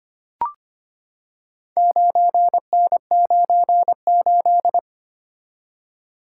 Morse code, heard as a keyed tone, about 25 wpm, 690 Hz. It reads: 9N98
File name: practice / 25wpm / Four Letter-Number Combinations - 25wpm